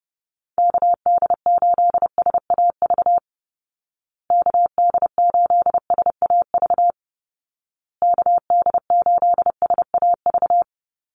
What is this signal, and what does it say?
Morse code, heard as a keyed tone, about 30 wpm, 700 Hz. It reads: KD8SAV KD8SAV KD8SAV
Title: Call Signs Repeated 2x - 30wpm